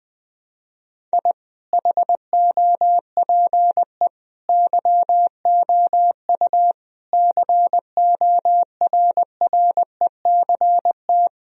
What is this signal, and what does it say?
Morse code, heard as a keyed tone, about 21 wpm, 700 Hz. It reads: I HOPE YOU CORRECT
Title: Sentences from Top 1000 Words - Review Part 2 - 20wpm